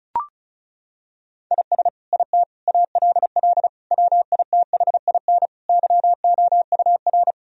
Morse code, heard as a keyed tone, about 36 wpm, 700 Hz. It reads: IS IT ALL WITHIN YOUR
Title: Sentences from Top 1000 Words - Review Part 2 - 35wpm